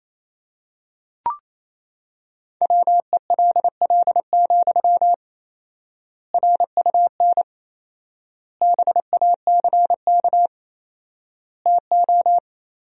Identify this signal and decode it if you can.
Morse code, heard as a keyed tone, about 28 wpm, 700 Hz. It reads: WELL, RUN BACK TO
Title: 4x Word-Spacing - Sentences from Top 300 Words - 28wpm